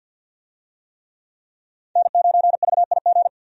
Morse code, heard as a keyed tone, about 48 wpm, 700 Hz. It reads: N9VIG
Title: USA Call Signs - 5 Characters - 50wpm